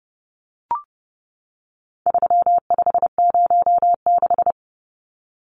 Morse code, heard as a keyed tone, about 30 wpm, 700 Hz. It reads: 3506
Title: Numbers - 4-Digit 30wpm